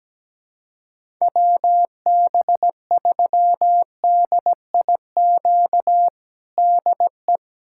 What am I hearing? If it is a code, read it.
Morse code, heard as a keyed tone, about 17 wpm, 700 Hz. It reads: WB3DIQ DE
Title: USA Callsign DE Callsign - 6 Characters - 17wpm